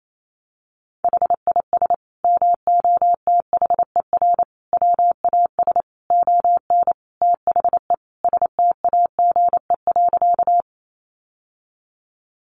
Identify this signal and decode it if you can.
Morse code, heard as a keyed tone, about 28 wpm, 700 Hz. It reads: HIS MOTHER WAS ON THE STAGE.
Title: Sentences from Top 1200 Words 28wpm